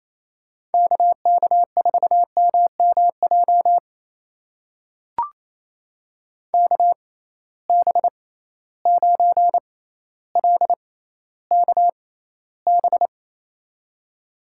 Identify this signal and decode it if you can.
Morse code, heard as a keyed tone, about 28 wpm, spaced as for 14 wpm, 700 Hz. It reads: KK4MMJ KB9LKB
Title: USA Call Signs - 6x Character Spacing - 28wpm